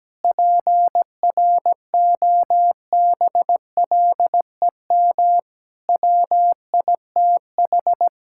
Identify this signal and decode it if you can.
Morse code, heard as a keyed tone, about 17 wpm, 700 Hz. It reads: PROBLEM WITH
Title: Sentences from Top 1000 Words - Review Part 6 - 17wpm